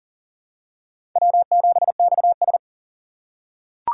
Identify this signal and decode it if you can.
Morse code, heard as a keyed tone, about 40 wpm, 700 Hz. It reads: W7XS
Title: USA Call Signs - 6x Character Spacing - 40wpm